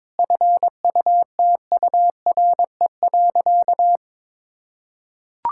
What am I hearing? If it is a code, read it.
Morse code, heard as a keyed tone, about 22 wpm, 700 Hz. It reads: FUTURE.
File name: Difficult Sentences with Hints - 22wpm